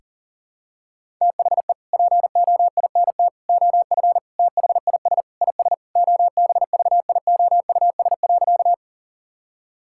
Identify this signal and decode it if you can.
Morse code, heard as a keyed tone, about 37 wpm, 700 Hz. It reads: THE POINT OF THIS IS OBVIOUS.
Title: Sentences from Top 1600 Words 40wpm